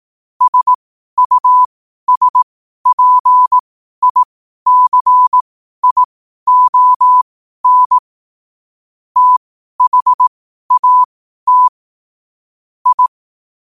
Morse code, heard as a keyed tone, about 18 wpm, spaced as for 12 wpm, 1000 Hz. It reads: SUSPICION THAT I
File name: qotdfiles / 12wpm